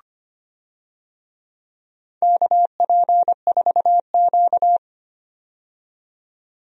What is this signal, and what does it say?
Morse code, heard as a keyed tone, about 25 wpm, 690 Hz. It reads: KP4Q
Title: DXCC-100 Call Signs - 4 Characters - 25wpm